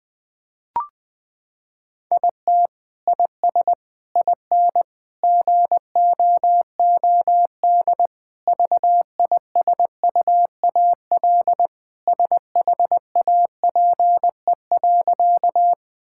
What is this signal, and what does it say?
Morse code, heard as a keyed tone, about 19 wpm, 700 Hz. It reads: IT IS IN GOOD VISUAL SHAPE.
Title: Sentences from Top 1000 Words - Review Part 6 - 20wpm